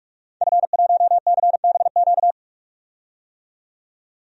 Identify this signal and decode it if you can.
Morse code, heard as a keyed tone, about 42 wpm, 700 Hz. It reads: F1CBQ
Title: DXCC-100 Call Signs - 5 Characters - 45wpm